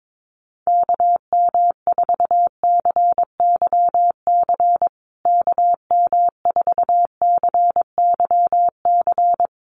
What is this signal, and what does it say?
Morse code, heard as a keyed tone, about 22 wpm, 700 Hz. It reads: KM4CYC KM4CYC
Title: USA Call Signs Repeated 3x 22wpm